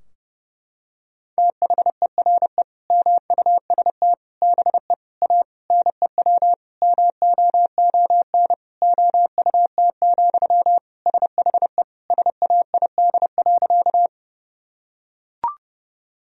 Morse code, heard as a keyed tone, about 32 wpm, 700 Hz. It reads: THERE MUST BE A NEW MOON OUT, SHE SAID.
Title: Sentences from Top 1000 Words - Review Part 3 - 30wpm